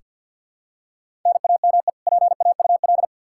Morse code, heard as a keyed tone, about 46 wpm, 700 Hz. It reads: NAME PAUL
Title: Practice QSO - Exchange 1 - Beginning - 50wpm